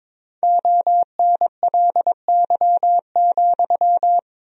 Morse code, heard as a keyed tone, about 22 wpm, 700 Hz. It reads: ONLY,
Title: 3x Word-Spacing - Sentences from Top 300 Words - 22wpm